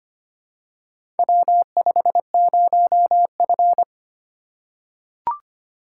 Morse code, heard as a keyed tone, about 25 wpm, 700 Hz. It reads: W50F
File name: practice / 25wpm / Four Letter-Number Combinations - 25wpm